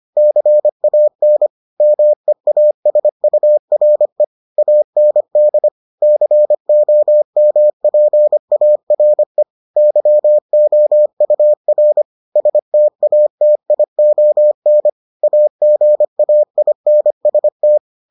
Morse code, heard as a keyed tone, about 25 wpm, 600 Hz. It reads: CAN MEASURE AND COMPARE YOUR STATION AGAINST